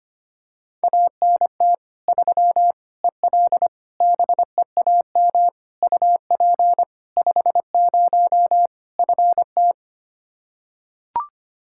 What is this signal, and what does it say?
Morse code, heard as a keyed tone, about 25 wpm, 700 Hz. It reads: ANT 3 EL BEAM UP 50 FT
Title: Practice QSO - Exchange 2 - Info - 25wpm